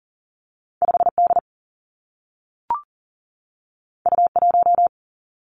Morse code, heard as a keyed tone, about 39 wpm, 700 Hz. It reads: 5 D U 1
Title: Two Challenging Letter-Number Combinations 40wpm